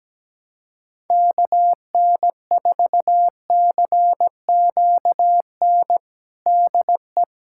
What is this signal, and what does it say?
Morse code, heard as a keyed tone, about 17 wpm, 700 Hz. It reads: KN4CQN DE
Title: USA Callsign DE Callsign - 6 Characters - 17wpm